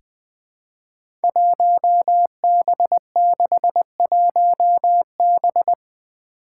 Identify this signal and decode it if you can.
Morse code, heard as a keyed tone, about 21 wpm, 700 Hz. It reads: 1B61B
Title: Five Challenging Letter-Number Combinations 20wpm